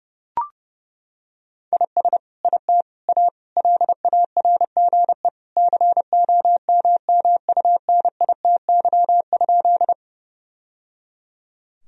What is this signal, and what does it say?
Morse code, heard as a keyed tone, about 30 wpm, 700 Hz. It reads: IS IT A LARGE COMMUNITY?
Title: Sentences from Top 1000 Words - Review Part 4 - 30wpm